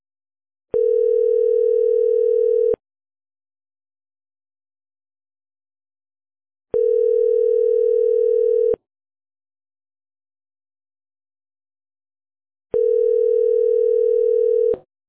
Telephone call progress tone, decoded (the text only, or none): ringback tone